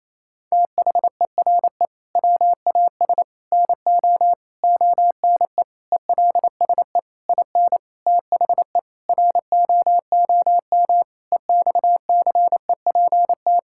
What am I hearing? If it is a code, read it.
Morse code, heard as a keyed tone, about 28 wpm, 700 Hz. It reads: THERE WAS NO ONE ELSE IN THE ROOM EXCEPT
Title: Sentences from Top 1600 Words 28wpm